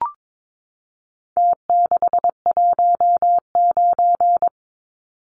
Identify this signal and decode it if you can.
Morse code, heard as a keyed tone, about 22 wpm, 700 Hz. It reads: T619
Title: Numbers - Contest Style - 4-Digit 22wpm